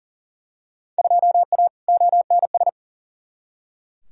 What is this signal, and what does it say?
Morse code, heard as a keyed tone, about 40 wpm, 700 Hz. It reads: 2A ONS
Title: Field Day Exchanges Encore - 40wpm